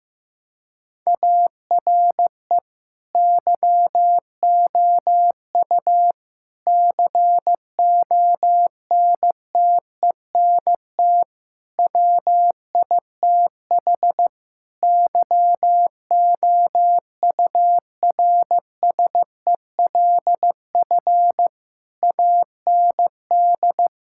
Morse code, heard as a keyed tone, about 15 wpm, 700 Hz. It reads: ARE YOU CONTENT WITH YOURSELF AND
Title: Sentences from Top 1000 Words - Review Part 6 - 15wpm